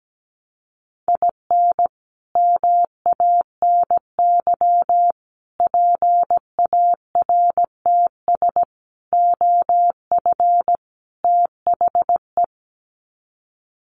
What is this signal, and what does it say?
Morse code, heard as a keyed tone, about 17 wpm, 700 Hz. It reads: IN MANY PARTS OF THE
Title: Sets of 5 Words Encore 17wpm